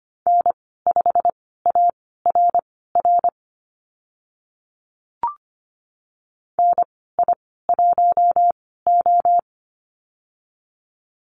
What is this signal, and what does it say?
Morse code, heard as a keyed tone, about 25 wpm, spaced as for 16 wpm, 700 Hz. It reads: N5ARR NI1O
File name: Lesson 15g - Number 5 - Call Signs 25wpm